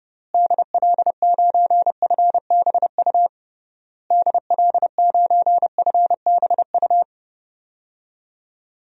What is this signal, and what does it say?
Morse code, heard as a keyed tone, about 30 wpm, 700 Hz. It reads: DL9FBU DL9FBU
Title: DXCC-100 Call Signs - 6 Characters - Repeated 3x - 30wpm